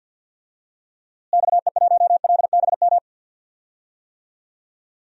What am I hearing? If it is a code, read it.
Morse code, heard as a keyed tone, about 52 wpm, 700 Hz. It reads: XE1LBM
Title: DXCC-100 Call Signs - 6 Characters - Repeated 2x - 50wpm